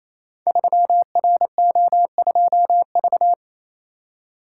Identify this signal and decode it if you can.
Morse code, heard as a keyed tone, about 29 wpm, 700 Hz. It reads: 3RO2V
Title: Code Groups - 5 Letter:Numbers - 28wpm